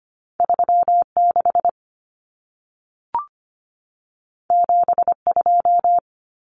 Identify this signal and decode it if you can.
Morse code, heard as a keyed tone, about 25 wpm, 700 Hz. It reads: 36 72
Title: Common Cut Numbers - 2-Digit 25wpm